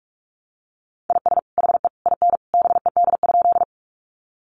Morse code, heard as a keyed tone, about 44 wpm, 700 Hz. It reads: IS HE IN BED?
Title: Sentences from Top 500 Words 45wpm